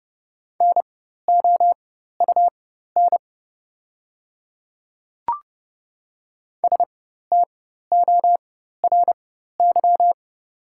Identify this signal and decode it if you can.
Morse code, heard as a keyed tone, about 31 wpm, spaced as for 14 wpm, 690 Hz. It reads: NOUN STORY
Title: 4x Character Spacing - Top 500 Words - 30wpm